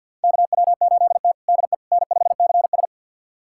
Morse code, heard as a keyed tone, about 50 wpm, 700 Hz. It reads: KW8T DE N5CS
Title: USA Callsign DE Callsign - 4 Characters - 50wpm